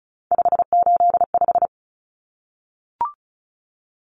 Morse code, heard as a keyed tone, about 34 wpm, 700 Hz. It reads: 585
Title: Common Cut Numbers - 3-Digit 35wpm